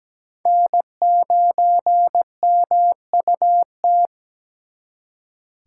Morse code, heard as a keyed tone, about 20 wpm, 700 Hz. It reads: N9MUT
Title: Speed Racing - USA 1x3 Call Signs 17wpm